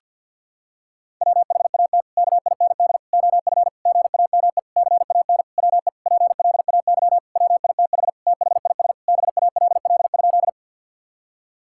Morse code, heard as a keyed tone, about 49 wpm, 700 Hz. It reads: WHAT KIND OF GAME CAN WE PLAY WITH THIS BALL?
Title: Sentences from Top 1000 Words - Review Part 6 - 50wpm